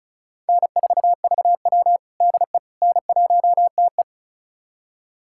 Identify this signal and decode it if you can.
Morse code, heard as a keyed tone, about 36 wpm, 700 Hz. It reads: N4VW DE N1TE
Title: USA Callsign DE Callsign - 4 Characters - 35wpm